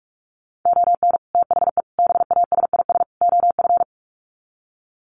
Morse code, heard as a keyed tone, about 44 wpm, 700 Hz. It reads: ON THE BASIS OF
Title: Sets of 4 Words 45wpm